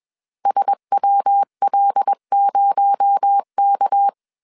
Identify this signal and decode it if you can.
Morse code, heard as a keyed tone, about 21 wpm, 800 Hz. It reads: SWL0K
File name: CW / DET / 1 6 1500C S A 20wpm 2048